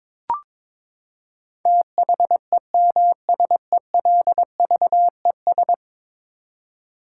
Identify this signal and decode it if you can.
Morse code, heard as a keyed tone, about 21 wpm, 700 Hz. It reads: THEMSELVES
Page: morse-code-ninja-practice-22wpm